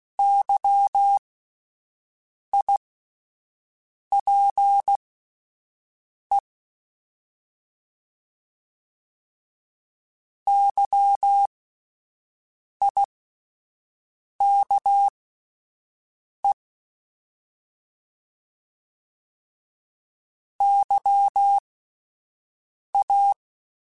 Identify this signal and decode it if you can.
Morse code, heard as a keyed tone, about 16 wpm, spaced as for 5 wpm, 800 Hz. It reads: YIPE YIKE YA
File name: Alphabetical / Alphabetical mp3s / Ys @5